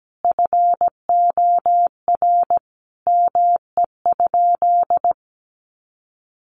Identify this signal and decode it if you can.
Morse code, heard as a keyed tone, about 17 wpm, 700 Hz. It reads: FOR ME?
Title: Sentences from Top 500 Words 17wpm